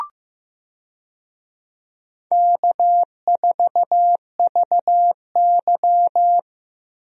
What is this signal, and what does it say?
Morse code, heard as a keyed tone, about 15 wpm, 700 Hz. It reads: K4VY
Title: USA Call Signs - 4 Characters - Repeated 2x - 15wpm